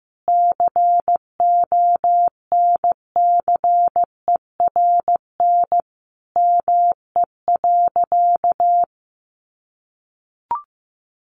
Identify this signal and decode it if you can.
Morse code, heard as a keyed tone, about 15 wpm, 700 Hz. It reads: CONCERN ME.
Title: Sentences from Top 1000 Words 15wpm